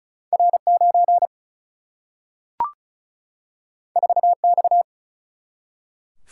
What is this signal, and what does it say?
Morse code, heard as a keyed tone, about 36 wpm, 700 Hz. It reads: R9 4X
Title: Single-Letter Single-Number 35wpm